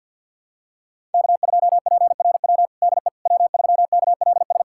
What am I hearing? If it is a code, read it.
Morse code, heard as a keyed tone, about 50 wpm, 700 Hz. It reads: K2PRW DE W3KLS
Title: USA Callsign DE Callsign - 5 Characters - 50wpm